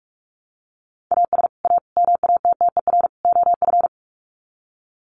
Morse code, heard as a keyed tone, about 44 wpm, 700 Hz. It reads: AS A MATTER OF
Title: Sets of 4 Words 45wpm